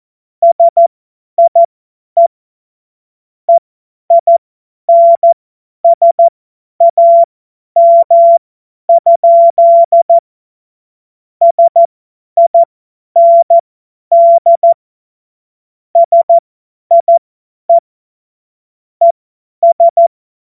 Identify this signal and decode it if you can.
Morse code, heard as a keyed tone, about 14 wpm, spaced as for 10 wpm, 680 Hz. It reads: SIE EINSAM? SIND SIE ES
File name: Morsekurs / Witze / Witz 23. Dezember 2025 50 F70